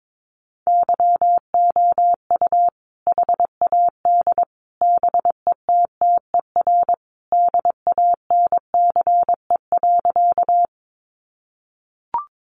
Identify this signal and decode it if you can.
Morse code, heard as a keyed tone, about 22 wpm, 700 Hz. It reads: YOU HAD BETTER DANCE.
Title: Sentences from Top 700 Words 22wpm